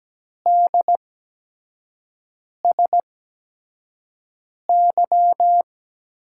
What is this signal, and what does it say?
Morse code, heard as a keyed tone, about 17 wpm, spaced as for 5 wpm, 700 Hz. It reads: DSY
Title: USA Call Signs - 6 Characters - 8x Character Spacing - 17wpm